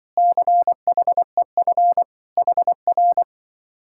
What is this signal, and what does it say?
Morse code, heard as a keyed tone, about 23 wpm, 700 Hz. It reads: CHEF HR
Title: Speed-Racing Common QSO Phrases - 20wpm